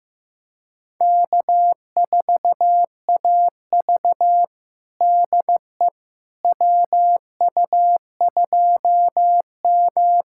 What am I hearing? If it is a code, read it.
Morse code, heard as a keyed tone, about 15 wpm, 700 Hz. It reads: K4AV DE WU2M
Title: USA Callsign DE Callsign - 4 Characters - 15wpm